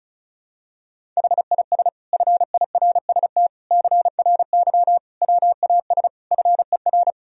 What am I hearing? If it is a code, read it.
Morse code, heard as a keyed tone, about 36 wpm, 700 Hz. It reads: HIS FIRST CRY WAS FER
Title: Sentences from Top 500 Words and Abbreviations 35wpm